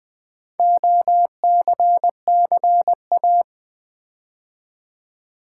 Morse code, heard as a keyed tone, about 20 wpm, 700 Hz. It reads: OCCA
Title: Build A Word - 4 Letters At A Time - 12-Letter Words - 20wpm